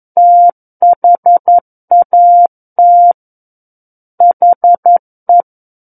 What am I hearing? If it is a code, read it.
Morse code, heard as a keyed tone, about 11 wpm, 700 Hz. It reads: THAT HE
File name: 11 WPM